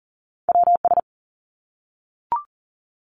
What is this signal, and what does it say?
Morse code, heard as a keyed tone, about 40 wpm, 700 Hz. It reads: WS
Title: Top Two Letter Combinations in Words 40wpm